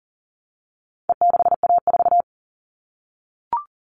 Morse code, heard as a keyed tone, about 40 wpm, 700 Hz. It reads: E6A4
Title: All Cut Numbers - 4-Digit 40wpm